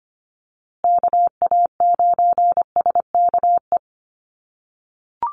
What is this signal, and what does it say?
Morse code, heard as a keyed tone, about 25 wpm, 700 Hz. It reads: KA9SKE
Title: USA Call Signs Repeated 2x 25wpm